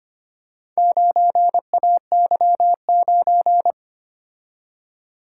Morse code, heard as a keyed tone, about 25 wpm, 700 Hz. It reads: 9AY9
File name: practice / 25wpm / Four Letter-Number Combinations - 25wpm